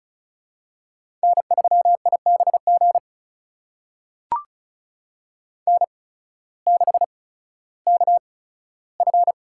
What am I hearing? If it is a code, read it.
Morse code, heard as a keyed tone, about 34 wpm, spaced as for 16 wpm, 700 Hz. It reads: N3IBG N6KF
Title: USA Call Signs - 5 Characters - 8x Character Spacing - 35wpm